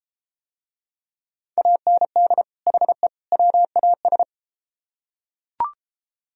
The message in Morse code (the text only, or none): AND HE WAS